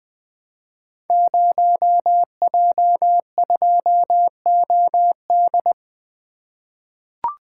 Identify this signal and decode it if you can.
Morse code, heard as a keyed tone, about 20 wpm, 700 Hz. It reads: 0J2OD
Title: Code Groups - 5 Letter:Numbers - 20wpm